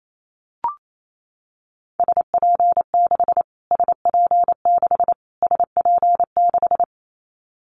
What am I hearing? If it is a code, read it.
Morse code, heard as a keyed tone, about 28 wpm, 700 Hz. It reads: SP6 SP6 SP6
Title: Callsign Prefixes for Top 25 DXCC - Repeated 3x - 28wpm